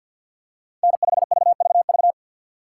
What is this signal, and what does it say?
Morse code, heard as a keyed tone, about 50 wpm, 700 Hz. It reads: N5VVV
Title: USA Call Signs - 5 Characters - 2x Character Spacing - 50wpm